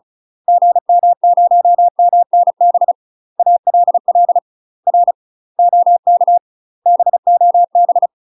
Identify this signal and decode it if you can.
Morse code, heard as a keyed tone, about 34 wpm, 700 Hz. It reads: GM0MNB ALL R OK BOB
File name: UK MASTER QSOs TRAINING / 35WPM / UK MASTER QSOs 0025